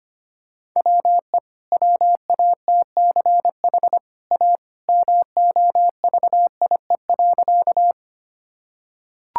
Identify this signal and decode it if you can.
Morse code, heard as a keyed tone, about 25 wpm, 690 Hz. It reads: WE WATCH A MOVIE.
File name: Easy Sentences with Hints - 25wpm